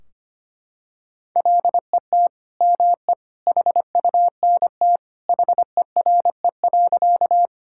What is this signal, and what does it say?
Morse code, heard as a keyed tone, about 25 wpm, 700 Hz. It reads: LET ME HUNT HERE.
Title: Sentences from Top 1600 Words 25wpm